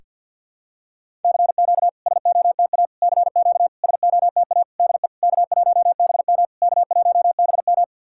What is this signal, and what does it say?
Morse code, heard as a keyed tone, about 50 wpm, 700 Hz. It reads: CQ SOTA CQ SOTA DE K1BM K1BM
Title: SOTA QSO Practice 50wpm